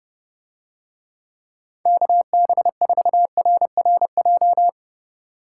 Morse code, heard as a keyed tone, about 32 wpm, 700 Hz. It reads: KB4RRJ